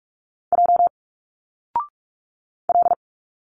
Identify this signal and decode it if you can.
Morse code, heard as a keyed tone, about 46 wpm, 700 Hz. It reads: J L
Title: Single Letters 45wpm